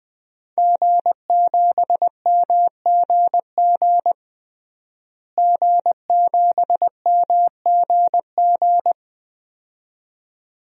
Morse code, heard as a keyed tone, about 21 wpm, 700 Hz. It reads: G7MGG G7MGG
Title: DXCC-100 Call Signs - 5 Characters - Repeated 2x - 20wpm